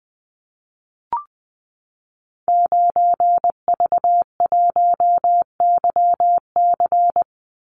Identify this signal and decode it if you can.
Morse code, heard as a keyed tone, about 20 wpm, 700 Hz. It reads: 9V1YC
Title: Call Signs - 20wpm